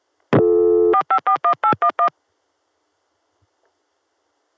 DTMF, heard as telephone touch tones, tones of about 90 ms. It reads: 4541811